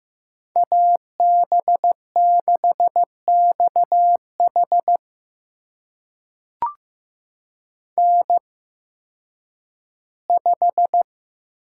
Morse code, heard as a keyed tone, about 15 wpm, spaced as for 10 wpm, 700 Hz. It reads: AB6XH N5
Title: USA Call Signs - 5 Characters - 8x Character Spacing - 15wpm